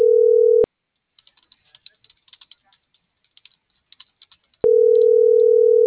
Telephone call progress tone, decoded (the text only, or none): ringback tone